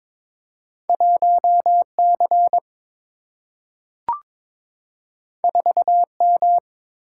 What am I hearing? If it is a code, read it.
Morse code, heard as a keyed tone, about 22 wpm, 700 Hz. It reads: 1C 4M
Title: Single-Letter Single-Number 22wpm